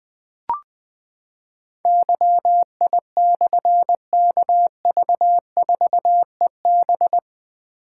Morse code, heard as a keyed tone, about 20 wpm, 700 Hz. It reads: YI/KV4EB
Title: Difficult Call Signs 20wpm